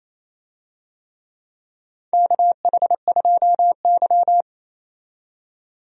Morse code, heard as a keyed tone, about 28 wpm, 700 Hz. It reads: KH2Y
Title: USA Call Signs - 4 Characters - 28wpm